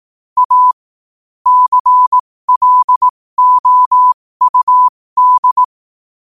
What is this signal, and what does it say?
Morse code, heard as a keyed tone, about 18 wpm, spaced as for 15 wpm, 1000 Hz. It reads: A CLOUD
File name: qotdfiles / 15wpm